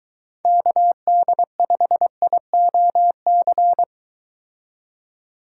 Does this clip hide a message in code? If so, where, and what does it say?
Yes: Morse code from 0.4 to 3.8 s, heard as a keyed tone, about 23 wpm, 700 Hz. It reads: KD5IOC